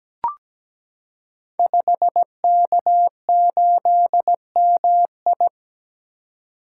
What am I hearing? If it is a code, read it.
Morse code, heard as a keyed tone, about 17 wpm, 700 Hz. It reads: 5K8MI
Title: Code Groups - 5 Letter:Numbers - 17wpm